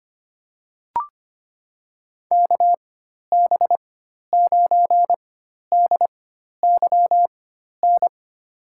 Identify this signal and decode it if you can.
Morse code, heard as a keyed tone, about 25 wpm, spaced as for 12 wpm, 700 Hz. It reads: KB9DYN
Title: USA Call Signs - 6 Characters - 4x Character Spacing - 25wpm